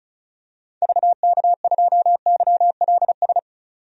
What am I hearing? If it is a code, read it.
Morse code, heard as a keyed tone, about 36 wpm, 700 Hz. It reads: VK2YLS
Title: DXCC-100 Call Signs - 6 Characters - 35wpm